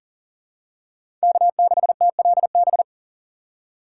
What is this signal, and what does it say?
Morse code, heard as a keyed tone, about 40 wpm, 700 Hz. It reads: K6TLB
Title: USA Call Signs - 5 Characters - 2x Character Spacing - 40wpm